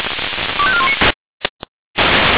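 Five-tone selective call (ZVEI), heard as quick runs of five tones, tones of about 70 ms. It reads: 25419
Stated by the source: ZVEI five-tone call received on an SDR, from the feuerwehr-orlamuende.org (fire brigade Orlamünde) recording server.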